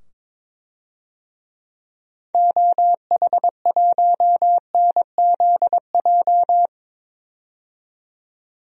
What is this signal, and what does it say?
Morse code, heard as a keyed tone, about 22 wpm, 700 Hz. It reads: OH1NZJ
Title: DXCC-100 Call Signs - 6 Characters - 22wpm